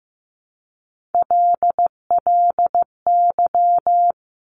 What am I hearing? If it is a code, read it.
Morse code, heard as a keyed tone, about 15 wpm, 700 Hz. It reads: LLY
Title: Top Three Letter Combinations in Words 15wpm